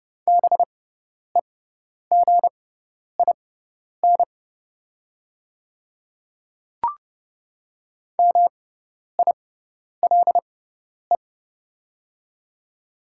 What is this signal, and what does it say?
Morse code, heard as a keyed tone, about 30 wpm, spaced as for 10 wpm, 700 Hz. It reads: BEGIN MILE
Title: 6x Character Spacing - Top 300 Words - 30wpm